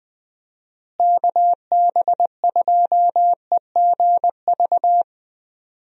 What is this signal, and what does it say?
Morse code, heard as a keyed tone, about 20 wpm, 700 Hz. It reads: KB2EGV
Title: USA Call Signs - 6 Characters - Repeated 3x - 20wpm